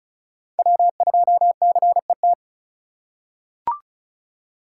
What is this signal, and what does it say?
Morse code, heard as a keyed tone, about 34 wpm, 700 Hz. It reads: W2CET
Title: USA Call Signs - 5 Characters - 2x Character Spacing - 35wpm